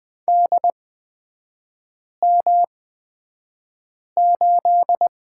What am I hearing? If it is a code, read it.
Morse code, heard as a keyed tone, about 20 wpm, spaced as for 5 wpm, 700 Hz. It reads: DM8
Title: 243 Lesson 39e - Letter z - Three Characters 20wpm